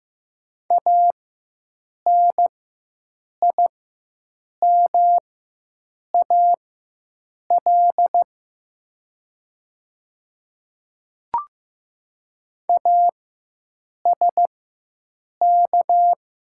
Morse code, heard as a keyed tone, about 15 wpm, spaced as for 7 wpm, 700 Hz. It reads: ANIMAL ASK
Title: 4x Character Spacing - Top 500 Words - 15wpm